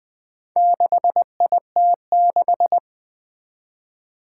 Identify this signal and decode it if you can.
Morse code, heard as a keyed tone, about 20 wpm, 700 Hz. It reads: 6IT6
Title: Four Challenging Letter-Number Combinations 20wpm